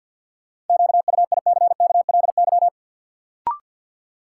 Morse code, heard as a keyed tone, about 52 wpm, 700 Hz. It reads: QUICKLY
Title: Sentences from Top 700 Words - New Word Drill - 50wpm